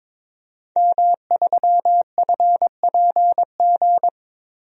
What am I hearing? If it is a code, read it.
Morse code, heard as a keyed tone, about 22 wpm, 700 Hz. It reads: M3FPG
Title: DXCC-100 Call Signs - 5 Characters - Repeated 3x - 22wpm